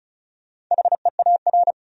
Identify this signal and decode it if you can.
Morse code, heard as a keyed tone, about 43 wpm, 700 Hz. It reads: HEAR
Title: Speed-Racing Top 500 Words 35wpm